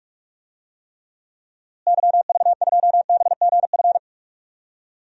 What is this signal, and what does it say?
Morse code, heard as a keyed tone, about 46 wpm, 700 Hz. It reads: YV2BGF
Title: DXCC-100 Call Signs - 6 Characters - Repeated 2x - 45wpm